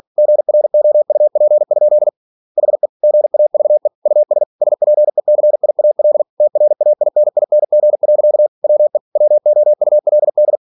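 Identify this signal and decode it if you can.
Morse code, heard as a keyed tone, about 46 wpm, 600 Hz. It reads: GROUP? HE GAVE US SPECIAL TRAINING. WE WOULD
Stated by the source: Practice text from an English novel sent at 47 WPM.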